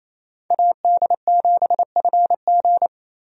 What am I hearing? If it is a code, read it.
Morse code, heard as a keyed tone, about 27 wpm, 700 Hz. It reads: AD7FG
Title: USA Call Signs - 5 Characters - 28wpm